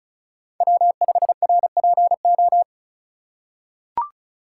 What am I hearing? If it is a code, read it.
Morse code, heard as a keyed tone, about 36 wpm, 700 Hz. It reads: W5RPO
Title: USA Call Signs - 5 Characters - 3x Character Spacing - 35wpm